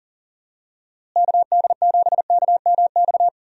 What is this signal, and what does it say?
Morse code, heard as a keyed tone, about 37 wpm, 700 Hz. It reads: KD7KMX
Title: USA Call Signs - 6 Characters - Repeated 2x - 40wpm